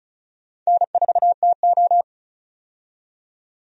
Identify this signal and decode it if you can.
Morse code, heard as a keyed tone, about 36 wpm, 700 Hz. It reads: N4TO
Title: USA Call Signs - 4 Characters - 35wpm